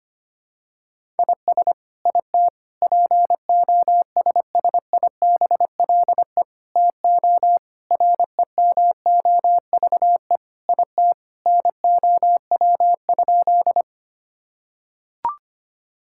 Morse code, heard as a keyed tone, about 25 wpm, 700 Hz. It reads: IS IT POSSIBLE TO REMOVE IT NOW?
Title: Sentences from Top 1000 Words - Review Part 5 - 25wpm